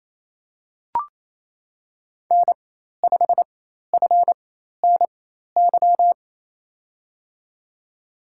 Morse code, heard as a keyed tone, about 28 wpm, spaced as for 13 wpm, 700 Hz. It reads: N5FNY
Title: USA Call Signs - 5 Characters - 4x Character Spacing - 28wpm